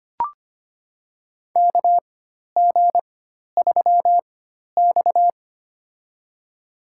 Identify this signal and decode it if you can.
Morse code, heard as a keyed tone, about 25 wpm, spaced as for 12 wpm, 700 Hz. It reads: KG3X
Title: USA Call Signs - 4x Character Spacing - 25wpm